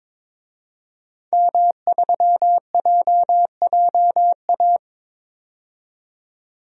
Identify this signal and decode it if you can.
Morse code, heard as a keyed tone, about 22 wpm, 700 Hz. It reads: M3JJA
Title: DXCC-100 Call Signs - 5 Characters - 22wpm